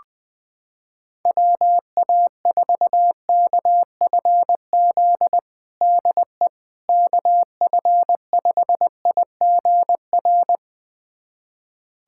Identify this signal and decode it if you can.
Morse code, heard as a keyed tone, about 21 wpm, 700 Hz. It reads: WA4KFZ DE KF5IGR
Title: USA Callsign DE Callsign - 6 Characters - 20wpm